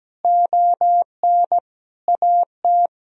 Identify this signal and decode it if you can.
Morse code, heard as a keyed tone, about 17 wpm, 700 Hz. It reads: ON AT
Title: Sentences from Top 1000 Words - Review Part 4 - 17wpm